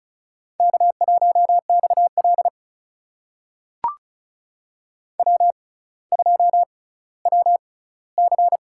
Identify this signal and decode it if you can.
Morse code, heard as a keyed tone, about 34 wpm, spaced as for 18 wpm, 700 Hz. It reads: K1XL W2WC
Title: USA Call Signs - 4 Characters - 6x Character Spacing - 35wpm